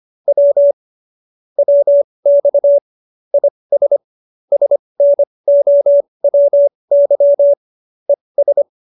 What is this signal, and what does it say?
Morse code, heard as a keyed tone, about 25 wpm, spaced as for 20 wpm, 570 Hz. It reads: W WX IS SNOWY ES
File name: qso220b 20